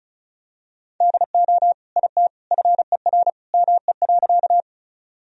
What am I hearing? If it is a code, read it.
Morse code, heard as a keyed tone, about 36 wpm, 700 Hz. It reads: DO IT FER ME.
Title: Sentences from Top 200 Words and Abbreviations 35wpm